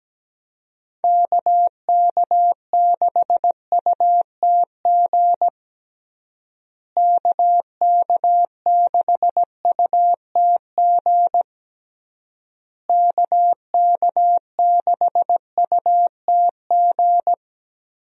Morse code, heard as a keyed tone, about 17 wpm, 700 Hz. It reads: KK6UTG KK6UTG KK6UTG
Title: USA Call Signs - 6 Characters - Repeated 3x - 17wpm